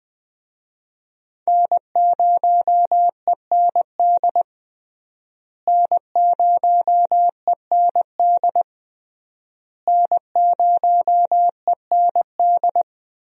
Morse code, heard as a keyed tone, about 20 wpm, 700 Hz. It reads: N0END N0END N0END
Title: USA Call Signs - 5 Characters - Repeated 3x - 20wpm